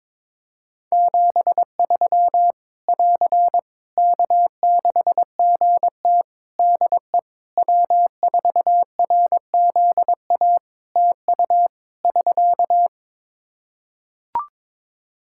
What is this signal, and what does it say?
Morse code, heard as a keyed tone, about 22 wpm, 700 Hz. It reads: 73 <AR> K6GT DE W4RZA TU <SK>
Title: Practice QSO - Exchange 4 - Ending - 22wpm